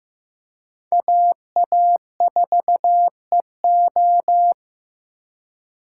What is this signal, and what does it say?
Morse code, heard as a keyed tone, about 18 wpm, 700 Hz. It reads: AA4EO
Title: Speed Racing - USA 2x2 Call Signs 15wpm